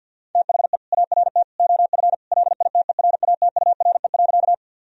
Morse code, heard as a keyed tone, about 50 wpm, 700 Hz. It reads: THE ART OF LITERATURE.
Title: Sentences from Top 1000 Words - Review Part 3 - 50wpm